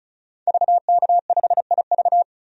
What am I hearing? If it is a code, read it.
Morse code, heard as a keyed tone, about 36 wpm, 700 Hz. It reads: VK5IV
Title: DXCC-100 Call Signs - 5 Characters - 35wpm